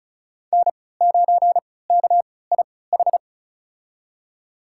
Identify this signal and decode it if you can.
Morse code, heard as a keyed tone, about 36 wpm, spaced as for 20 wpm, 700 Hz. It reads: N9KIH
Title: USA Call Signs - 5 Characters - 3x Character Spacing - 35wpm